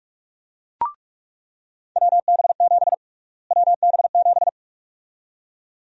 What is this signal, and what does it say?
Morse code, heard as a keyed tone, about 46 wpm, 700 Hz. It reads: WB7 WB7
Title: USA Call Sign Prefixes - Repeated 2x - 45wpm